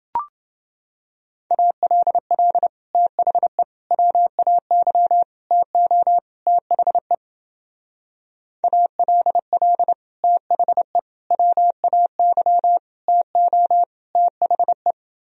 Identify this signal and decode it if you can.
Morse code, heard as a keyed tone, about 28 wpm, 700 Hz. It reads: ALL THE WAY TO THE ALL THE WAY TO THE